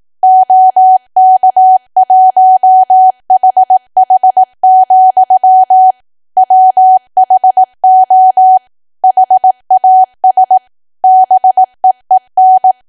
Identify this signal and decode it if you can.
Morse code, heard as a keyed tone, about 18 wpm, 750 Hz. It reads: OK1HH, WHO HAS BEEN
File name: cwa / W1AW-Level2 / 18WPM